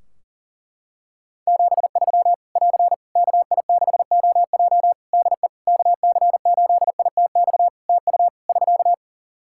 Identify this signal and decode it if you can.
Morse code, heard as a keyed tone, about 40 wpm, 700 Hz. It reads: 73 <AR> KI6OJ DE KC8ITX TU <SK>
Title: K1USN SST QSO Practice 40wpm